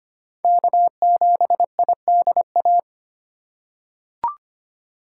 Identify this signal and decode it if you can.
Morse code, heard as a keyed tone, about 25 wpm, 700 Hz. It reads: K7IDA
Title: USA Call Signs - 5 Characters - 2x Character Spacing - 25wpm